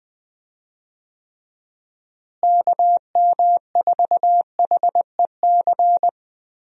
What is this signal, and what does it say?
Morse code, heard as a keyed tone, about 20 wpm, 700 Hz. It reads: KM4HEC
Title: USA Call Signs - 6 Characters - Repeated 2x - 20wpm